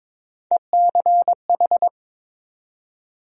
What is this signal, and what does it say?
Morse code, heard as a keyed tone, about 22 wpm, 700 Hz. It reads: ECH
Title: USA Call Sign Postfixes - 3 Letters 22wpm